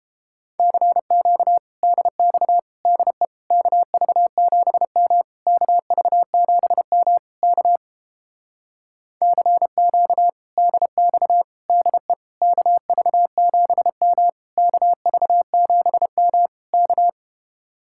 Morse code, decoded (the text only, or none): CQ DX DE KV7M KV7M K CQ DX DE KV7M KV7M K